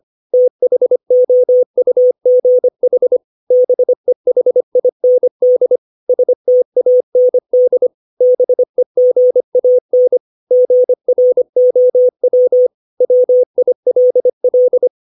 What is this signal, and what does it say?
Morse code, heard as a keyed tone, about 25 wpm, 500 Hz. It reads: THOUGH BEHIND STAND BEGAN GROW WILL
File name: 632 words / 632 Words 25WPM